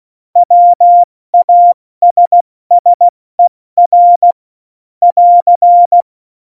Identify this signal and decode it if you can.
Morse code, heard as a keyed tone, about 16 wpm, 700 Hz. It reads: WASSER <AR>
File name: Morsekurs / Witze / Witz 12. Dezember 2017, 70 F80